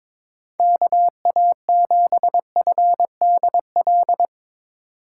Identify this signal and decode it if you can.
Morse code, heard as a keyed tone, about 24 wpm, 700 Hz. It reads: KA7FDL
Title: Speed Racing - USA Call Signs 22wpm